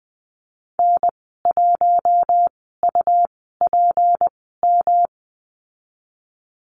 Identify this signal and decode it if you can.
Morse code, heard as a keyed tone, about 20 wpm, spaced as for 15 wpm, 700 Hz. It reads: N1UPM
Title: 154 Lesson 25g - Letter p - Call Signs 20wpm